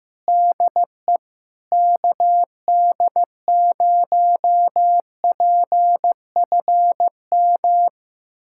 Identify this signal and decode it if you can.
Morse code, heard as a keyed tone, about 15 wpm, 700 Hz. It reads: DE KD0PFM
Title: USA Callsign DE Callsign - 6 Characters - 15wpm